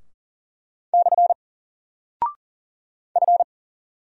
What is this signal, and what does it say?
Morse code, heard as a keyed tone, about 38 wpm, 700 Hz. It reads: / F
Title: Single Character (Letter:Number:Punctuation:Prosign) Alt 40wpm